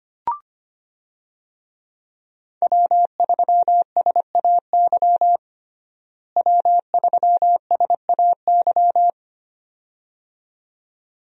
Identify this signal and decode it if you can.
Morse code, heard as a keyed tone, about 25 wpm, 700 Hz. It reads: W3SAY W3SAY
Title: USA Call Signs - 5 Characters - Repeated 2x - 25wpm